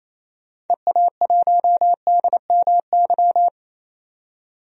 Morse code, heard as a keyed tone, about 27 wpm, 700 Hz. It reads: EA1DMY
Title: DXCC-100 Call Signs - 6 Characters - 28wpm